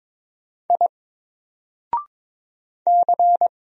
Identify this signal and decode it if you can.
Morse code, heard as a keyed tone, about 22 wpm, 700 Hz. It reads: I C
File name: Single-Letter-Number-22wpm